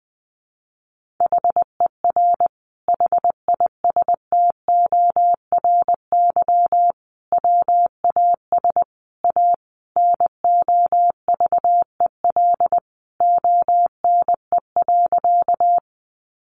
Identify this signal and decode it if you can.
Morse code, heard as a keyed tone, about 20 wpm, 700 Hz. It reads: HER HISTORY WAS A NOVEL ONE.
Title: Sentences from Top 1300 Words 20wpm